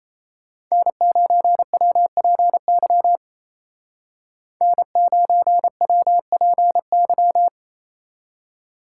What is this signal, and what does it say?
Morse code, heard as a keyed tone, about 36 wpm, 700 Hz. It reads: N9WPY N9WPY